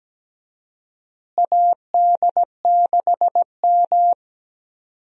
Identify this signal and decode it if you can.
Morse code, heard as a keyed tone, about 17 wpm, 700 Hz. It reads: AD6M
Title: USA Call Signs - 4 Characters - Repeated 2x - 17wpm